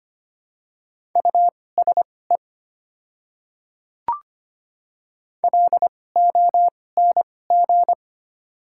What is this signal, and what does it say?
Morse code, heard as a keyed tone, about 25 wpm, spaced as for 18 wpm, 700 Hz. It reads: USE LONG
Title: 2x Character Spacing - Top 100 Words - 25wpm